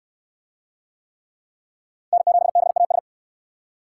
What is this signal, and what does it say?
Morse code, heard as a keyed tone, about 69 wpm, 700 Hz. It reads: N7BAS